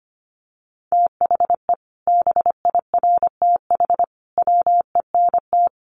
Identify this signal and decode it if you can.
Morse code, heard as a keyed tone, about 25 wpm, 700 Hz. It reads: THE BIRTH WENT
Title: Sentences from Top 800 Words 25wpm